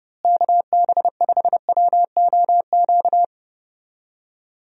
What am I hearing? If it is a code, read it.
Morse code, heard as a keyed tone, about 30 wpm, 700 Hz. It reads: KB5WOQ
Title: USA Call Signs - 6 Characters - Repeated 3x - 30wpm